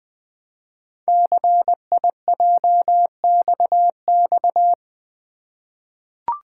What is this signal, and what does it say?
Morse code, heard as a keyed tone, about 20 wpm, 700 Hz. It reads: CIJXX
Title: Code Groups - 5 Letters - 20wpm